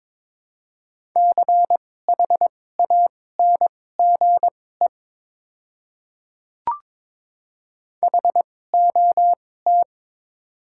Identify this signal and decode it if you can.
Morse code, heard as a keyed tone, about 22 wpm, spaced as for 16 wpm, 700 Hz. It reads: CHANGE HOT
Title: 2x Character Spacing - Top 200 Words - 22wpm